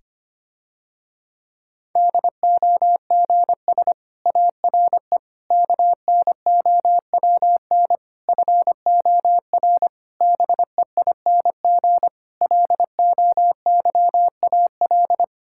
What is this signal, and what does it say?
Morse code, heard as a keyed tone, about 25 wpm, 700 Hz. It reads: DOGS ARE KNOWN FOR BEING LOYAL
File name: practice / 25wpm / Difficult Sentences with Hints - 25wpm